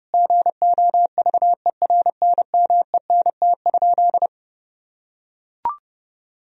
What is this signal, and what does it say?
Morse code, heard as a keyed tone, about 32 wpm, 700 Hz. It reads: GOVERNMENT?
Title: Sentences from Top 1000 Words - Review Part 4 - 30wpm